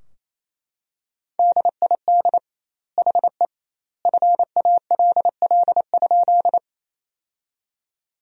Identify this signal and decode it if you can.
Morse code, heard as a keyed tone, about 28 wpm, 700 Hz. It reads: DID HE FALL?
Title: 2x Word-Spacing - Sentences from Top 500 Words - 28wpm